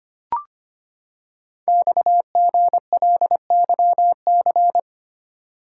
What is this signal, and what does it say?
Morse code, heard as a keyed tone, about 25 wpm, 700 Hz. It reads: XGLYC
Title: Code Groups - 5 Letters - 25wpm